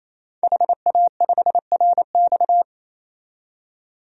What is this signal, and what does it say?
Morse code, heard as a keyed tone, about 28 wpm, 700 Hz. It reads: HA5RX
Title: DXCC-100 Call Signs - 5 Characters - 28wpm